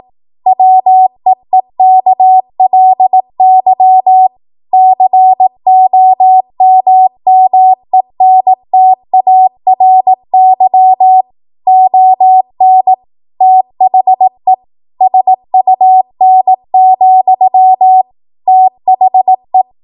Morse code, heard as a keyed tone, about 18 wpm, 750 Hz. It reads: WEEKLY COMMENTARY ON THE SUN, THE
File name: Morse / 18WPM